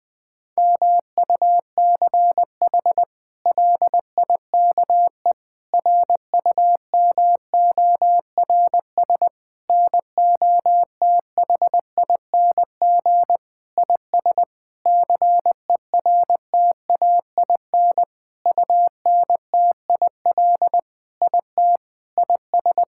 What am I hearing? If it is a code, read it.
Morse code, heard as a keyed tone, about 20 wpm, 700 Hz. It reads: MUCH LIKE RUMORS NOTHING IS CERTAIN UNTIL IT IS
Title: Sentences with 8-Word Phrases - Spoken Beginning - 20wpm